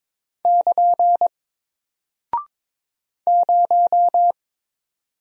Morse code, heard as a keyed tone, about 22 wpm, 700 Hz. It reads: <KN> 0
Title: Single Character (Letter:Number:Punctuation:Prosign) Alt 22wpm